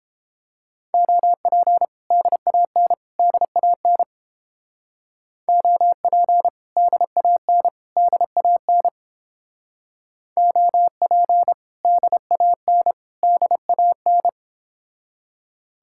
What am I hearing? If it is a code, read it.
Morse code, heard as a keyed tone, about 31 wpm, 700 Hz. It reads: OP DAN DAN OP DAN DAN OP DAN DAN